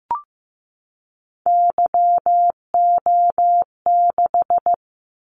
Morse code, heard as a keyed tone, about 15 wpm, 700 Hz. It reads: YO6
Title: Callsign Prefixes for Top 25 DXCC - 15wpm